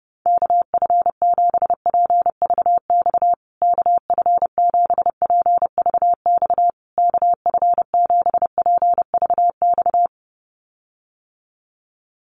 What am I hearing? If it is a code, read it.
Morse code, heard as a keyed tone, about 30 wpm, 700 Hz. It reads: KF7PVX KF7PVX KF7PVX
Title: USA Call Signs Repeated 3x 30wpm